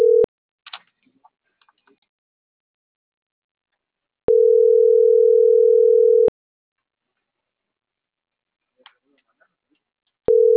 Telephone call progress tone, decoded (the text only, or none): ringback tone